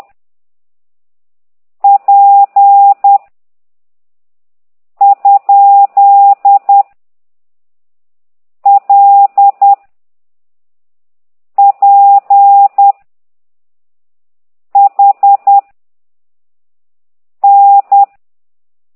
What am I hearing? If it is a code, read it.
Morse code, heard as a keyed tone, about 10 wpm, 800 Hz. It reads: P ? L P H N
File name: ct2idw / exerc9